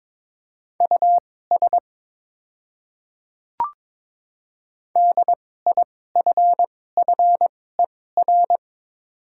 Morse code, heard as a keyed tone, about 23 wpm, spaced as for 16 wpm, 690 Hz. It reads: US DIFFER
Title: 2x Character Spacing - Top 200 Words - 22wpm